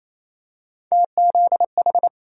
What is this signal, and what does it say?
Morse code, heard as a keyed tone, about 28 wpm, 700 Hz. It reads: TZH
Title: USA Call Sign Postfixes - 3 Letters 28wpm